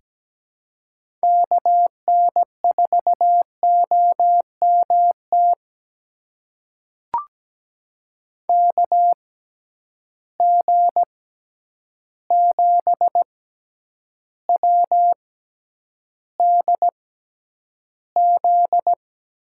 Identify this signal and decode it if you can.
Morse code, heard as a keyed tone, about 17 wpm, spaced as for 9 wpm, 700 Hz. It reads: KN4OMT KG7WDZ
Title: USA Call Signs - 6 Characters - 6x Character Spacing - 17wpm